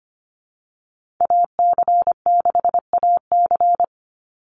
Morse code, heard as a keyed tone, about 25 wpm, 700 Hz. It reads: AC6AC
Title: Call Signs - 25wpm